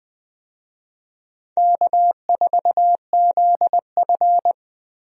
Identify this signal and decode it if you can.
Morse code, heard as a keyed tone, about 20 wpm, 700 Hz. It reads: K4ZF
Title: USA Call Signs - 4 Characters - 20wpm